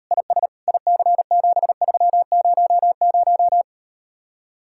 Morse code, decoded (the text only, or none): IS IC7300